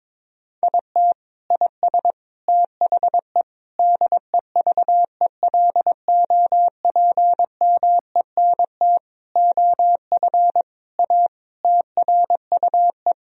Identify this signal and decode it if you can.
Morse code, heard as a keyed tone, about 22 wpm, 700 Hz. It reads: IT IS THE DEVELOPMENT OF A TRUE
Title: Sentences from Top 1000 Words - Review Part 4 - 22wpm